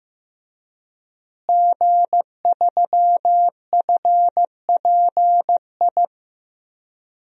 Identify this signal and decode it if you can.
Morse code, heard as a keyed tone, about 15 wpm, 700 Hz. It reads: G3FPI
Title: DXCC-100 Call Signs - 5 Characters - Repeated 2x - 15wpm